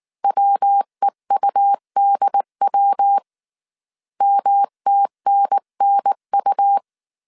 Morse code, heard as a keyed tone, about 19 wpm, 800 Hz. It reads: WEUDW MTNNU